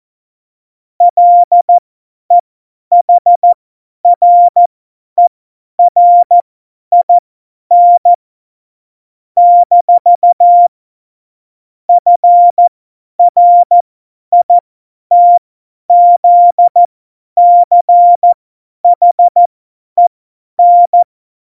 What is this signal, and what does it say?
Morse code, heard as a keyed tone, about 14 wpm, spaced as for 10 wpm, 700 Hz. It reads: LEHRERIN - FRITZCHEN
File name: Morsekurs / Witze / Witz 19. Januar 2021, 50 F70